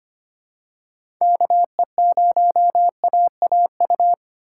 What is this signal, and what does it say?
Morse code, heard as a keyed tone, about 25 wpm, 700 Hz. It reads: KE0AAU
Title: USA Call Signs - 6 Characters - 2x Character Spacing - 25wpm